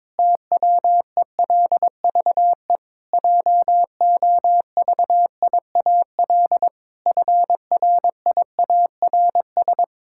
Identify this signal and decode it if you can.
Morse code, heard as a keyed tone, about 22 wpm, 700 Hz. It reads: TWELVE JOVIAL FRIARS
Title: Pangram Challenge v2 - 22wpm